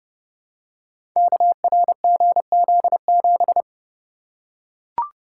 Morse code, heard as a keyed tone, about 28 wpm, 700 Hz. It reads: KRGZ7
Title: Code Groups - 5 Letter:Numbers - 30wpm